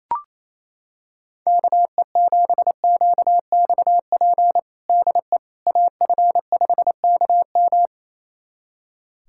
Morse code, heard as a keyed tone, about 28 wpm, 700 Hz. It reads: KE7QXP DE AF5KM
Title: USA Callsign DE Callsign 28wpm